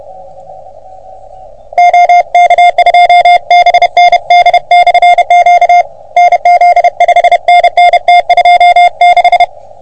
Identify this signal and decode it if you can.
Morse code, heard as a keyed tone, about 30 wpm, 720 Hz. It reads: OK2BND/Q NZ 5NN T26
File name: OM3PA PILUP CUC 388